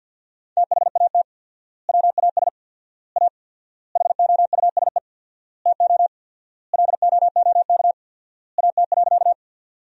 Morse code, heard as a keyed tone, about 51 wpm, 700 Hz. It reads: THAT WAS A HOUSE TO LOOK AT.
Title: 4x Word-Spacing - Sentences from Top 200 Words - 50wpm